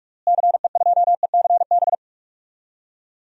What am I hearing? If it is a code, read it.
Morse code, heard as a keyed tone, about 44 wpm, 700 Hz. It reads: CE2ECB
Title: DXCC-100 Call Signs - 6 Characters - Repeated 3x - 45wpm